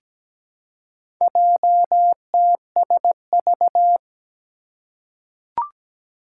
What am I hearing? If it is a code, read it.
Morse code, heard as a keyed tone, about 17 wpm, 700 Hz. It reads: JTSV
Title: Four Challenging Letter-Number Combinations 17wpm